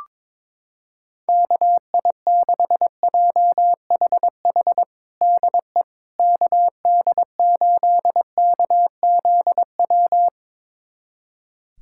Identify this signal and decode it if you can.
Morse code, heard as a keyed tone, about 22 wpm, 700 Hz. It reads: KI6JHH DE KD8KZW
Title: USA Callsign DE Callsign 22wpm